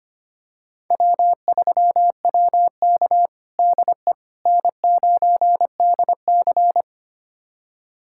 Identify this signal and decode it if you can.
Morse code, heard as a keyed tone, about 25 wpm, 700 Hz. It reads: W3WK DE N9DC
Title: USA Callsign DE Callsign - 4 Characters - 25wpm